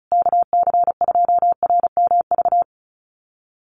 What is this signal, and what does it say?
Morse code, heard as a keyed tone, about 34 wpm, 700 Hz. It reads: KC2RMV
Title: USA Call Signs Repeated 3x 35wpm